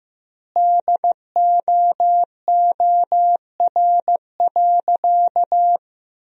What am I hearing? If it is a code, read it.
Morse code, heard as a keyed tone, about 15 wpm, 700 Hz. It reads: DOOR.
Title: 2x Word-Spacing - Sentences from Top 500 Words - 15wpm